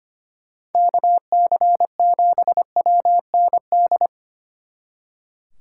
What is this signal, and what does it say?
Morse code, heard as a keyed tone, about 31 wpm, 700 Hz. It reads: KC7WND